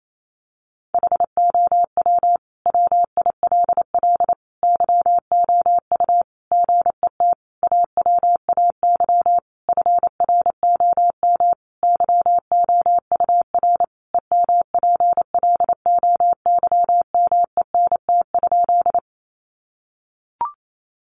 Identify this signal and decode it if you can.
Morse code, heard as a keyed tone, about 28 wpm, 700 Hz. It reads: HOW WILL YOU GET AWAY FROM YOUR EMPLOYMENT?
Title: Sentences from Top 1500 Words 28wpm